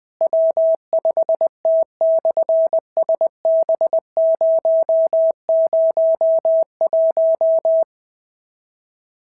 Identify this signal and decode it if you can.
Morse code, heard as a keyed tone, about 20 wpm, 650 Hz. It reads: W5T/SB001
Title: SOTA QSO Practice 20wpm